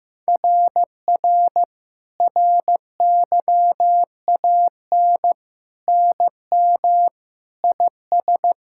Morse code, heard as a keyed tone, about 15 wpm, 700 Hz. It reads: RR RYAN NM IS
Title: SOTA QSO Practice 15wpm